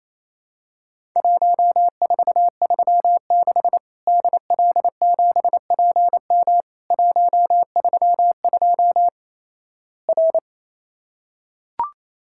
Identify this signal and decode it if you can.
Morse code, heard as a keyed tone, about 28 wpm, 700 Hz. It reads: 1436 DL7PM 132 R
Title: Worked All Europe DX Contest - QTC Exchange - 28wpm